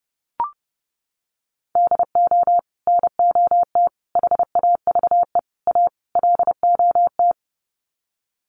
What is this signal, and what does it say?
Morse code, heard as a keyed tone, about 28 wpm, 700 Hz. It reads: DO NOT HAVE A LOT
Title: Sets of 5 Words Encore 30wpm